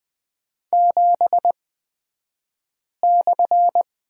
Morse code, heard as a keyed tone, about 20 wpm, spaced as for 5 wpm, 700 Hz. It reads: 7/
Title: 254 Lesson 41d - Period - Two Characters 20wpm